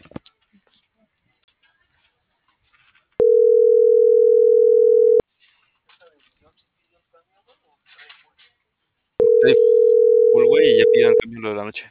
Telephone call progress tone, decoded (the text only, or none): ringback tone